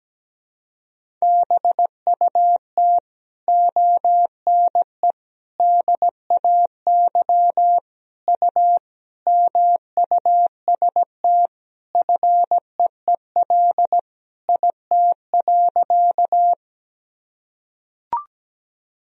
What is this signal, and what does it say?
Morse code, heard as a keyed tone, about 17 wpm, 700 Hz. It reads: BUT ONE DAY U MUST FEEL IT.
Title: Sentences from Top 400 Words and Abbreviations 17wpm